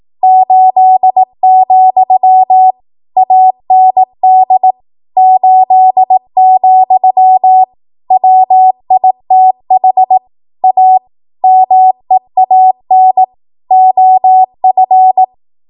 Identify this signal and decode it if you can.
Morse code, heard as a keyed tone, about 18 wpm, 750 Hz. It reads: 8, AND 8, WITH A MEAN OF
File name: Archive / 18 WPM / 18WPM